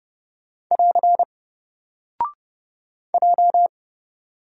Morse code, heard as a keyed tone, about 29 wpm, 700 Hz. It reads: <AR> J
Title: Single Character (Letter:Number:Punctuation:Prosign) 30wpm